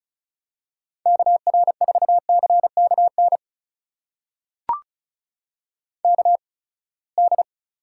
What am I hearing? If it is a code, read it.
Morse code, heard as a keyed tone, about 36 wpm, spaced as for 24 wpm, 700 Hz. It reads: KR4CKN KD
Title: USA Call Signs - 6 Characters - 8x Character Spacing - 35wpm